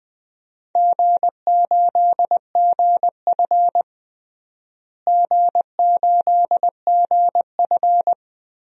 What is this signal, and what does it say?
Morse code, heard as a keyed tone, about 21 wpm, 700 Hz. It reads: G8GF G8GF
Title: DXCC-100 Call Signs - 4 Characters - Repeated 2x - 20wpm